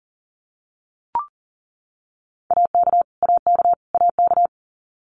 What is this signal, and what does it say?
Morse code, heard as a keyed tone, about 40 wpm, 700 Hz. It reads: AK AK AK
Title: US States Repeated 3x 40wpm